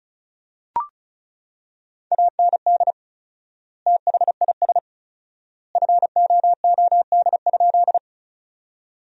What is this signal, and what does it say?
Morse code, heard as a keyed tone, about 36 wpm, 700 Hz. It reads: AND THIS FOOD?
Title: 4x Word-Spacing - Sentences from Top 300 Words - 35wpm